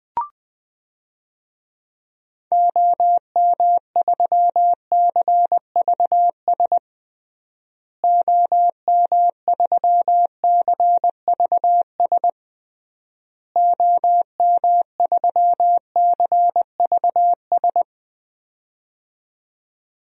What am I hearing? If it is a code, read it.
Morse code, heard as a keyed tone, about 20 wpm, 700 Hz. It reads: OM3CVS OM3CVS OM3CVS
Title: DXCC-100 Call Signs - 6 Characters - Repeated 3x - 20wpm